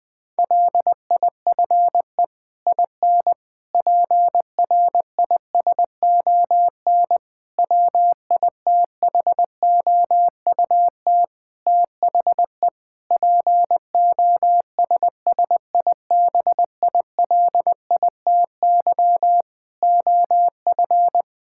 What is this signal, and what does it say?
Morse code, heard as a keyed tone, about 20 wpm, 700 Hz. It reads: LIFE IN PRISON WITHOUT THE POSSIBILITY OF
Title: Sets of 8 Words - 20wpm